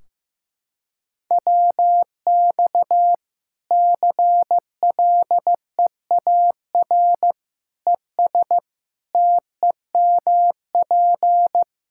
Morse code, heard as a keyed tone, about 15 wpm, 700 Hz. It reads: WX CLEAR ES TEMP
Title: Practice QSO - Exchange 2 - Info - 15wpm